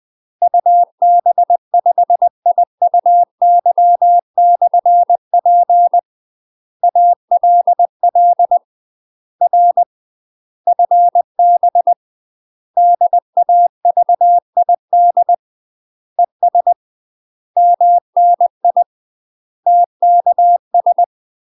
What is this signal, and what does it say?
Morse code, heard as a keyed tone, about 20 wpm, 700 Hz. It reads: UB5IUY/P ALL R FB DAVID ES MNI TKS
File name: UK MASTER QSOs TRAINING / 20WPM / UK MASTER QSOs 0070